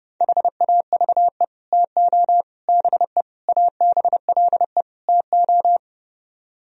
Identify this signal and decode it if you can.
Morse code, heard as a keyed tone, about 36 wpm, 700 Hz. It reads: HAVE TO BE ABLE TO
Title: Speed-Racing - Sets of 5 Words 30wpm